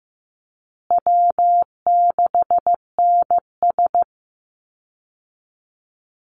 Morse code, heard as a keyed tone, about 15 wpm, 700 Hz. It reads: W6NS
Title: Easy Call Signs Repeated 2x - 15wpm